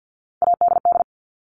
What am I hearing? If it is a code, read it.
Morse code, heard as a keyed tone, about 48 wpm, 700 Hz. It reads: ADD
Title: Top 300 Words 50wpm